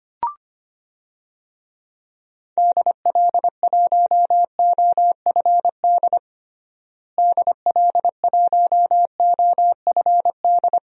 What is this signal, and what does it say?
Morse code, heard as a keyed tone, about 25 wpm, 700 Hz. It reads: DL1OFD DL1OFD
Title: DXCC-100 Call Signs - 6 Characters - Repeated 3x - 25wpm